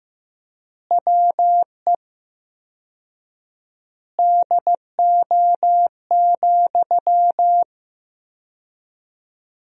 Morse code, heard as a keyed tone, about 16 wpm, 700 Hz. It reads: WE DO,
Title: 4x Word-Spacing - Sentences from Top 100 Words - 15wpm